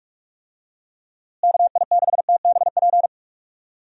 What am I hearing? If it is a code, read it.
Morse code, heard as a keyed tone, about 45 wpm, 700 Hz. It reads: KI6TBP
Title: USA Call Signs - 6 Characters - Repeated 2x - 45wpm